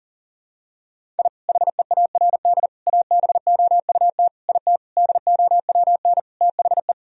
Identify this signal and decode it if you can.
Morse code, heard as a keyed tone, about 40 wpm, 700 Hz. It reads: I HEARD ABOUT IT DOWN THE
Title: Sentences from Top 1000 Words - Review Part 1 - 40wpm